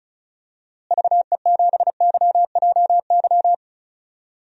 Morse code, heard as a keyed tone, about 34 wpm, 700 Hz. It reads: VE7YJY
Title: DXCC-100 Call Signs - 6 Characters - Repeated 2x - 35wpm